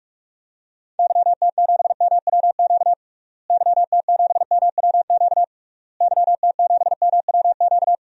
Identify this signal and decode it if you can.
Morse code, heard as a keyed tone, about 41 wpm, 700 Hz. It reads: YT7MWQ YT7MWQ YT7MWQ
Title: DXCC-100 Call Signs - 6 Characters - Repeated 3x - 45wpm